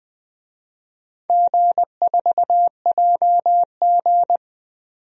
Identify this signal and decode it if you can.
Morse code, heard as a keyed tone, about 20 wpm, 700 Hz. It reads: G4JG
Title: DXCC-100 Call Signs - 4 Characters - 20wpm